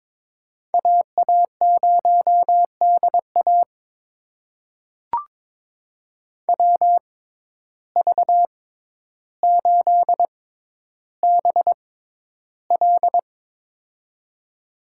Morse code, heard as a keyed tone, about 21 wpm, spaced as for 11 wpm, 700 Hz. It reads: AA0DA WV8BL
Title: USA Call Signs - 5 Characters - 6x Character Spacing - 22wpm